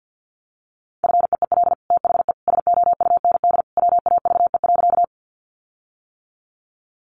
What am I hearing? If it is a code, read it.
Morse code, heard as a keyed tone, about 50 wpm, 700 Hz. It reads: FEEL THE SOUND WAVE.
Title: Sentences from Top 1400 Words 50wpm